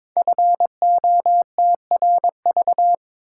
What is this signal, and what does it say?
Morse code, heard as a keyed tone, about 22 wpm, 700 Hz. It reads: FOTRV
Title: Code Groups - 5 Letters - 22wpm